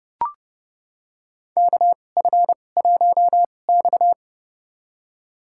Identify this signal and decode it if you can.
Morse code, heard as a keyed tone, about 29 wpm, spaced as for 21 wpm, 700 Hz. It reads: KF1X
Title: USA Call Signs - 4 Characters - 2x Character Spacing - 30wpm